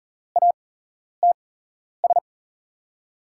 Morse code, heard as a keyed tone, about 40 wpm, spaced as for 11 wpm, 700 Hz. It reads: ATS
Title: USA Call Signs - 5 Characters - 8x Character Spacing - 40wpm